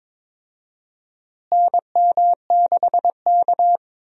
Morse code, heard as a keyed tone, about 22 wpm, 700 Hz. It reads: NM6K
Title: USA Call Signs - 4 Characters - Repeated 2x - 22wpm